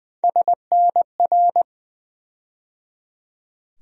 Morse code, heard as a keyed tone, about 20 wpm, 700 Hz. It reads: SNR
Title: USA Call Sign Postfixes - 3 Letters 20wpm